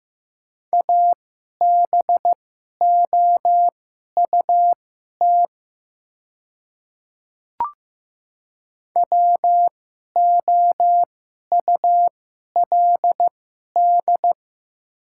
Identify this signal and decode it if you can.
Morse code, heard as a keyed tone, about 15 wpm, spaced as for 11 wpm, 700 Hz. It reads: ABOUT WOULD
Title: 2x Character Spacing - Top 200 Words - 15wpm